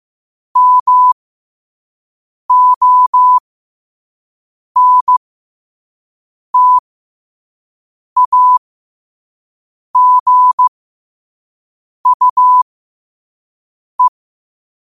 Morse code, heard as a keyed tone, about 15 wpm, spaced as for 5 wpm, 1000 Hz. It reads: MONTAGUE